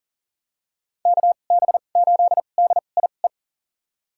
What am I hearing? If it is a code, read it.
Morse code, heard as a keyed tone, about 40 wpm, spaced as for 29 wpm, 700 Hz. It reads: KB8DIE
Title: USA Call Signs - 6 Characters - 2x Character Spacing - 40wpm